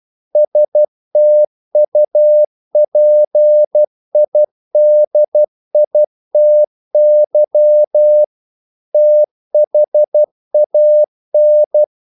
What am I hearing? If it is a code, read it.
Morse code, heard as a keyed tone, about 12 wpm, 600 Hz. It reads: STUPIDITY THAN
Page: English quotations in CW (telegraphy - morse)